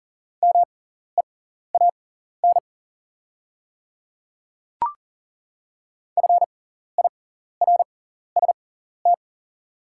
Morse code, heard as a keyed tone, about 40 wpm, spaced as for 14 wpm, 700 Hz. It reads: MEAN FIRST
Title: 6x Character Spacing - Top 200 Words - 40wpm